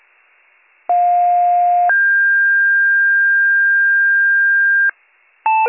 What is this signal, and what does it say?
Two-tone page, heard as a one-second tone, then a three-second tone, about 710 then 1600 Hz. It is heard from 0.9 to 4.9 s.